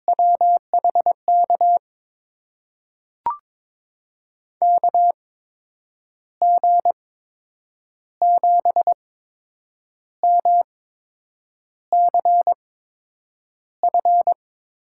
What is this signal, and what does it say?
Morse code, heard as a keyed tone, about 22 wpm, spaced as for 8 wpm, 700 Hz. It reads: WHK KG7MCF
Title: USA Call Signs - 6 Characters - 8x Character Spacing - 22wpm